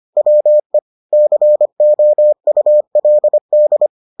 Morse code, heard as a keyed tone, about 25 wpm, 600 Hz. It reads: WE COULD